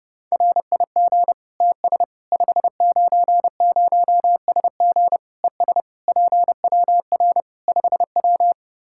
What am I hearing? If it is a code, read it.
Morse code, heard as a keyed tone, about 30 wpm, 700 Hz. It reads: RIG TS 590SG ES PWR 5W
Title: Practice QSO - Exchange 2 - Info - 30wpm